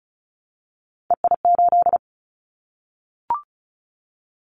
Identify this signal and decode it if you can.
Morse code, heard as a keyed tone, about 36 wpm, 690 Hz. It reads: EI8
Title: Callsign Prefixes for Top 50 DXCC - 35wpm